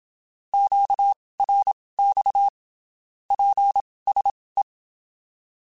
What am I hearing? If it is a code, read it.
Morse code, heard as a keyed tone, about 27 wpm, spaced as for 19 wpm, 800 Hz. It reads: QRX PSE
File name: (spaced) / QSOs Spaced-mp3s / kc7haa @25 sp